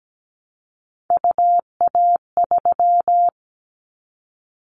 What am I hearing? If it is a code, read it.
Morse code, heard as a keyed tone, about 17 wpm, 700 Hz. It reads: UA3
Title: Callsign Prefixes for Top 100 DXCC - 17wpm